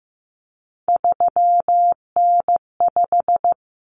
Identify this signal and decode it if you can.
Morse code, heard as a keyed tone, about 15 wpm, 700 Hz. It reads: 3N5
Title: Common Cut Numbers - 3-Digit 15wpm